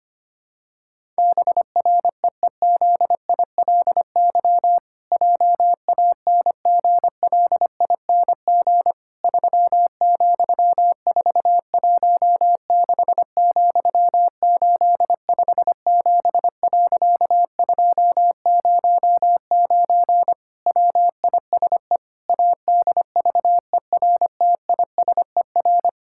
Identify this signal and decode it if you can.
Morse code, heard as a keyed tone, about 25 wpm, 700 Hz. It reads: BREEZILY JANGLING 3,416,857.209 WISE ADVERTISER
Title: Pangram Challenge v2 - 25wpm